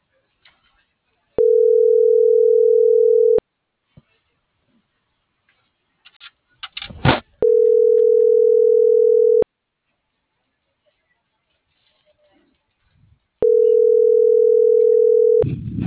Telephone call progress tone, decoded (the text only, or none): ringback tone